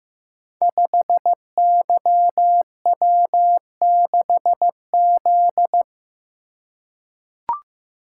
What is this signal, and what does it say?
Morse code, heard as a keyed tone, about 15 wpm, 700 Hz. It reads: 5YW6Z
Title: Code Groups - 5 Letter:Numbers - 15wpm